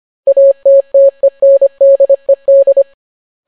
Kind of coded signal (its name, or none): Morse code